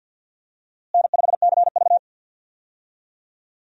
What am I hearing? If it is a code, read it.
Morse code, heard as a keyed tone, about 51 wpm, 700 Hz. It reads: N5CV
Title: USA Call Signs - 4 Characters - Repeated 3x - 50wpm